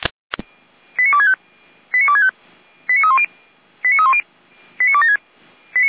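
Five-tone selective call (ZVEI), heard as quick runs of five tones, tones of about 70 ms. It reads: 89265 89265 89310 89310 89276
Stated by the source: ZVEI five-tone selective call 89174, received on an SDR.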